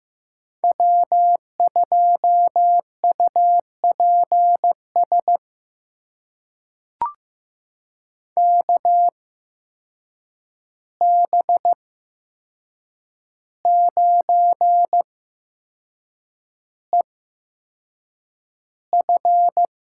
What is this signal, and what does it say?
Morse code, heard as a keyed tone, about 15 wpm, spaced as for 6 wpm, 700 Hz. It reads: W2UPS KB9EF
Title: USA Call Signs - 5 Characters - 8x Character Spacing - 15wpm